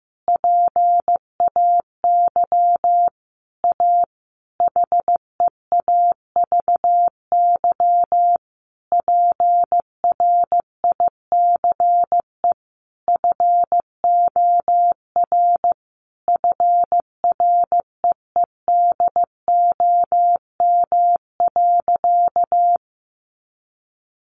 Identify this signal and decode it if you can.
Morse code, heard as a keyed tone, about 15 wpm, 700 Hz. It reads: PAY A HEAVY PRICE FOR FREEDOM.
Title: Sentences from Top 1100 Words 15wpm